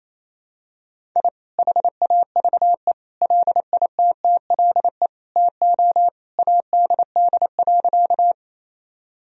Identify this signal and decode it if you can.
Morse code, heard as a keyed tone, about 28 wpm, 700 Hz. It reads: I HAVE LITTLE TO ADD.
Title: Sentences from Top 1000 Words - Review Part 1 - 28wpm